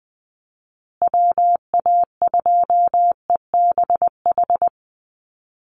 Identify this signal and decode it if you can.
Morse code, heard as a keyed tone, about 20 wpm, 700 Hz. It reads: WA2EBH
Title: USA Call Signs Repeated 2x 20wpm